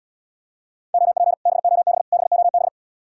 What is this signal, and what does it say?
Morse code, heard as a keyed tone, about 78 wpm, 700 Hz. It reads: OP BOB BOB